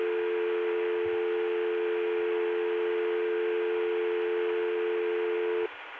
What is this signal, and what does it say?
Dial tone (North America), 350 + 440 Hz.